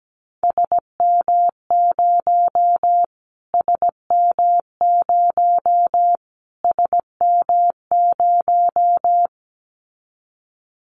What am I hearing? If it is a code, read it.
Morse code, heard as a keyed tone, about 17 wpm, 700 Hz. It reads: SM0 SM0 SM0
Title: Callsign Prefixes for Top 25 DXCC - Repeated 3x - 17wpm